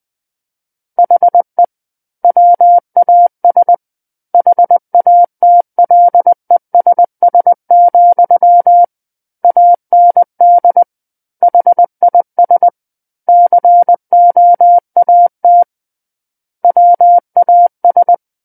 Morse code, heard as a keyed tone, about 20 wpm, 700 Hz. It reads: HE WAS HATLESS, AND HIS COAT WAS
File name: 20 WPM